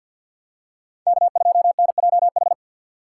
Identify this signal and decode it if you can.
Morse code, heard as a keyed tone, about 47 wpm, 700 Hz. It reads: K2NJH
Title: USA Call Signs - 5 Characters - 6x Character Spacing - 50wpm